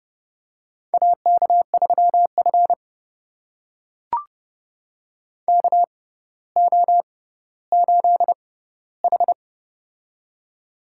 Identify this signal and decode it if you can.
Morse code, heard as a keyed tone, about 30 wpm, spaced as for 15 wpm, 700 Hz. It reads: AK3F KO8H
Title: USA Call Signs - 6x Character Spacing - 30wpm